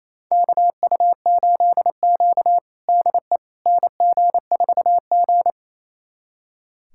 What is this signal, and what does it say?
Morse code, heard as a keyed tone, about 27 wpm, 700 Hz. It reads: KU8Q DE NG4G
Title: USA Callsign DE Callsign - 4 Characters - 28wpm